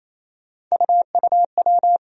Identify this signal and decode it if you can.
Morse code, heard as a keyed tone, about 28 wpm, 700 Hz. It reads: UUW
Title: USA Call Sign Postfixes - 3 Letters 28wpm